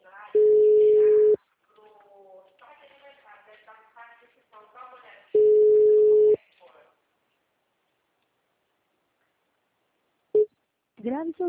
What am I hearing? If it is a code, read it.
Ringback tone (Europe, 425 Hz), with speech.
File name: out- -2012- - - .28